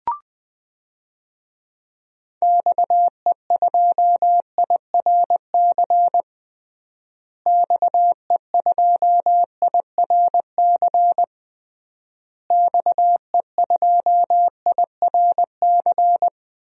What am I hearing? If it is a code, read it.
Morse code, heard as a keyed tone, about 21 wpm, 700 Hz. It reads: XE2IRC XE2IRC XE2IRC
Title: DXCC-100 Call Signs - 6 Characters - Repeated 3x - 20wpm